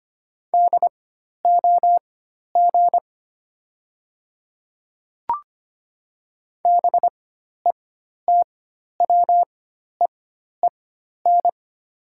Morse code, heard as a keyed tone, about 25 wpm, spaced as for 12 wpm, 700 Hz. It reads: DOG BETWEEN
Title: 4x Character Spacing - Top 500 Words - 25wpm